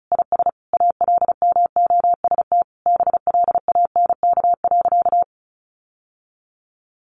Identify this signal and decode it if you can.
Morse code, heard as a keyed tone, about 34 wpm, 700 Hz. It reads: IS ALMOST BLANK.
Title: Sentences from Top 1500 Words 35wpm